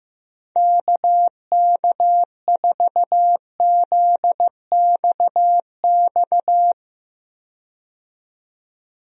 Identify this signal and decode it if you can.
Morse code, heard as a keyed tone, about 15 wpm, 700 Hz. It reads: KK4ZXX
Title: USA Call Signs - 6 Characters - 15wpm